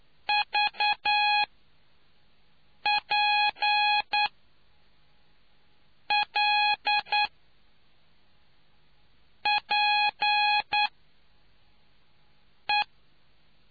Morse code, heard as a keyed tone, about 9 wpm, spaced as for 4 wpm, 830 Hz. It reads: VPLPE